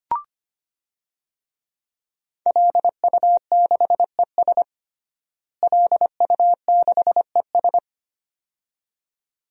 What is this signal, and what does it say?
Morse code, heard as a keyed tone, about 25 wpm, 700 Hz. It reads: LU6ES LU6ES
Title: DXCC-100 Call Signs - 5 Characters - Repeated 2x - 25wpm